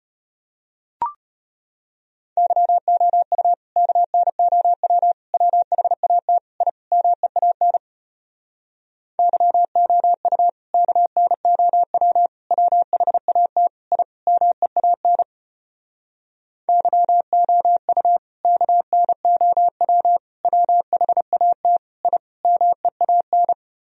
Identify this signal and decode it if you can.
Morse code, heard as a keyed tone, about 33 wpm, 700 Hz. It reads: YOU KNOW WHAT I MEAN YOU KNOW WHAT I MEAN YOU KNOW WHAT I MEAN